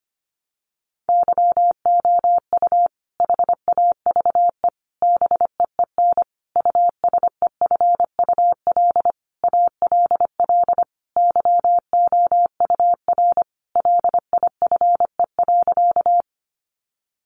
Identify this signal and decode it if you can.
Morse code, heard as a keyed tone, about 25 wpm, 700 Hz. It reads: YOU HAVE BEEN USEFUL ALL YOUR LIFE.
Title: Sentences from Top 700 Words 25wpm